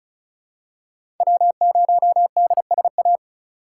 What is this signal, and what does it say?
Morse code, heard as a keyed tone, about 36 wpm, 700 Hz. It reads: W0DSA
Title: USA Call Signs - 5 Characters - 4x Character Spacing - 35wpm